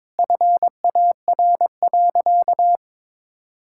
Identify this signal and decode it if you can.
Morse code, heard as a keyed tone, about 22 wpm, 700 Hz. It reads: FAR.
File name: practice / 22wpm / Sentences with 2-Word Phrases - 22wpm